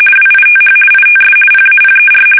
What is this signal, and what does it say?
RTTY, about 50 baud. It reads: WITH ITS PAPER
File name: fsk850-50B / teletype-corp